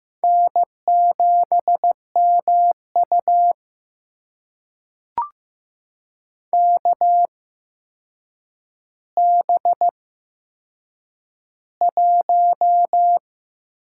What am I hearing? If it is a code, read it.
Morse code, heard as a keyed tone, about 15 wpm, spaced as for 7 wpm, 700 Hz. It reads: N7MU KB1
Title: USA Call Signs - 8x Character Spacing - 15wpm